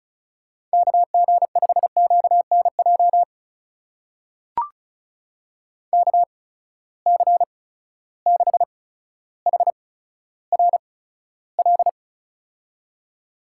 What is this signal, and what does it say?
Morse code, heard as a keyed tone, about 36 wpm, spaced as for 15 wpm, 700 Hz. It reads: KG5QNJ KC6HRL
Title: USA Call Signs - 6 Characters - 8x Character Spacing - 35wpm